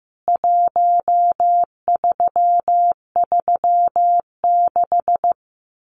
Morse code, heard as a keyed tone, about 15 wpm, 700 Hz. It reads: 1336
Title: CWT Exchange 15wpm